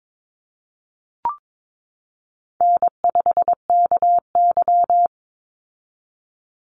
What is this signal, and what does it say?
Morse code, heard as a keyed tone, about 22 wpm, 700 Hz. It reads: N5KY
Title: Easy Call Signs - 22wpm